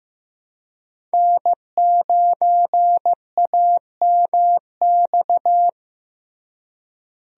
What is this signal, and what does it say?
Morse code, heard as a keyed tone, about 15 wpm, 700 Hz. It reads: N9AMX
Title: USA Call Signs - 5 Characters - Repeated 3x - 15wpm